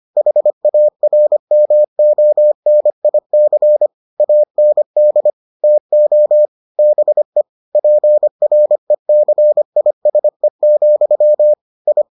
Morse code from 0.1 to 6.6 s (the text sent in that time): HARMONIC AND TO